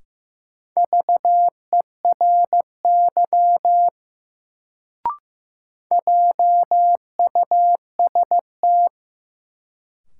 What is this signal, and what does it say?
Morse code, heard as a keyed tone, about 15 wpm, 700 Hz. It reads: VERY JUST
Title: Sets of 2 Words - New Word Drill - 15wpm